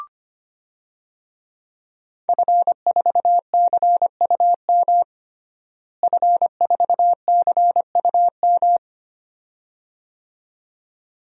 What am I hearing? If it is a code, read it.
Morse code, heard as a keyed tone, about 25 wpm, 700 Hz. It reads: F4CUM F4CUM
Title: DXCC-100 Call Signs - 5 Characters - Repeated 2x - 25wpm